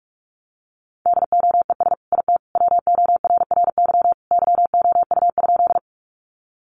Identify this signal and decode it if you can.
Morse code, heard as a keyed tone, about 44 wpm, 700 Hz. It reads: DOES IT WORRY YOU?
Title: Sentences from Top 1100 Words 45wpm